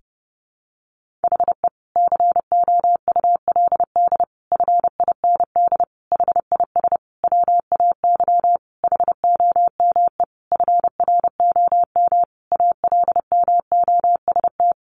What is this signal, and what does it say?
Morse code, heard as a keyed tone, about 30 wpm, 700 Hz. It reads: HE COULD FIND HIS WAY HOME FROM ALMOST
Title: Sentences from Top 1400 Words 30wpm